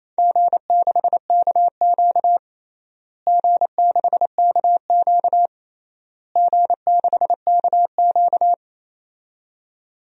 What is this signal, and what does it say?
Morse code, heard as a keyed tone, about 28 wpm, 700 Hz. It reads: G6KQ G6KQ G6KQ
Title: DXCC-100 Call Signs - 4 Characters - Repeated 3x - 28wpm